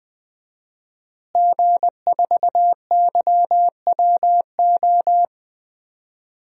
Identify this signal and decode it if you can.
Morse code, heard as a keyed tone, about 19 wpm, 700 Hz. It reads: G4YWO
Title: DXCC-100 Call Signs - 5 Characters - Repeated 3x - 20wpm